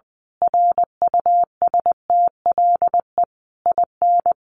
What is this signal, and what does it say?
Morse code, heard as a keyed tone, about 20 wpm, 700 Hz. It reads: RUSTLE IN
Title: A Sound Of Thunder 20wpm